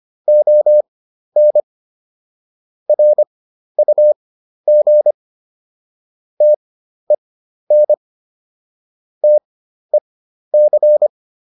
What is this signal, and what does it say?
Morse code, heard as a keyed tone, about 25 wpm, spaced as for 12 wpm, 600 Hz. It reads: ON RUG TEN TEC